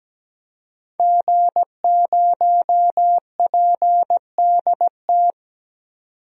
Morse code, heard as a keyed tone, about 17 wpm, 700 Hz. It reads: G0PDT
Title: DXCC-100 Call Signs - 5 Characters - 17wpm